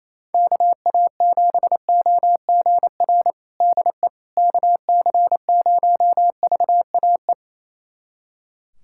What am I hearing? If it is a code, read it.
Morse code, heard as a keyed tone, about 28 wpm, 700 Hz. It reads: KA7OGR DE KC0VAE
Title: USA Callsign DE Callsign - 6 Characters - 28wpm